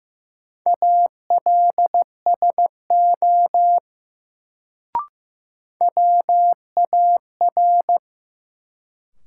Morse code, heard as a keyed tone, about 15 wpm, 700 Hz. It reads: ALSO WAR
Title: Sentences from Top 400 Words - New Word Drill - 15wpm